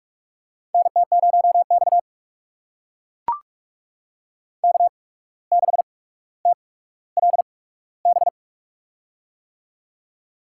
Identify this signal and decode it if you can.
Morse code, heard as a keyed tone, about 44 wpm, spaced as for 18 wpm, 700 Hz. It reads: NT0X K6TLB
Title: USA Call Signs - 8x Character Spacing - 45wpm